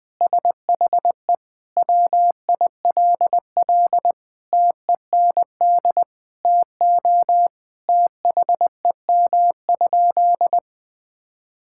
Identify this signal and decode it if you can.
Morse code, heard as a keyed tone, about 21 wpm, 700 Hz. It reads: SHE WILL TEND TO THEM?
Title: Sentences from Top 1000 Words - Review Part 5 - 20wpm